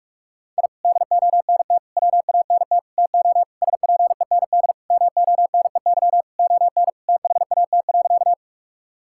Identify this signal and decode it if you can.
Morse code, heard as a keyed tone, about 44 wpm, 700 Hz. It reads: I DONT WANT TO SPEND MONEY ON THAT.
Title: Sentences from Top 1000 Words - Review Part 5 - 45wpm